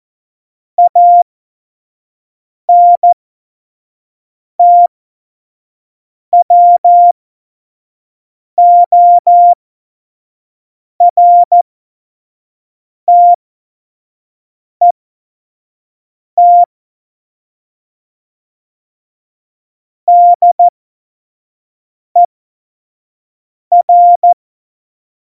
Morse code, heard as a keyed tone, about 14 wpm, spaced as for 5 wpm, 700 Hz. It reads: ANTWORTET DER
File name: Morsekurs / Witze / Witz 06. Februar 2018, 25 F70